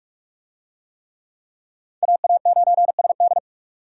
Morse code, heard as a keyed tone, about 40 wpm, 700 Hz. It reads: AA9SD